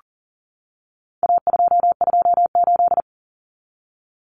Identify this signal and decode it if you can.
Morse code, heard as a keyed tone, about 40 wpm, 700 Hz. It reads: A228
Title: Common Cut Numbers - 4-Digit 40wpm